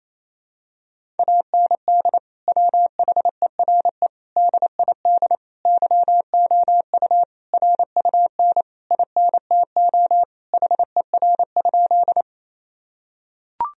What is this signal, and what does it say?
Morse code, heard as a keyed tone, about 28 wpm, 700 Hz. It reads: AND WHERE DID YOU RUN INTO HER?
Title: Sentences from Top 1000 Words - Review Part 1 - 28wpm